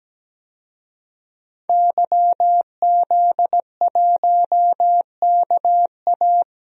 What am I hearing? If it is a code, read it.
Morse code, heard as a keyed tone, about 17 wpm, 700 Hz. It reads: YZ1KA
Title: DXCC-100 Call Signs - 5 Characters - 17wpm